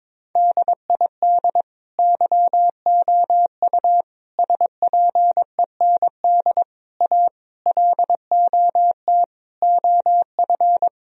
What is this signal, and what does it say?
Morse code, heard as a keyed tone, about 22 wpm, 700 Hz. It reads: DID YOU SPEND A LOT OF
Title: Sentences from Top 1000 Words - Review Part 4 - 22wpm